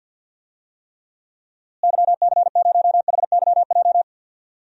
Morse code, heard as a keyed tone, about 48 wpm, 700 Hz. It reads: YC0HYJ
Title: DXCC-100 Call Signs - 6 Characters - Repeated 2x - 50wpm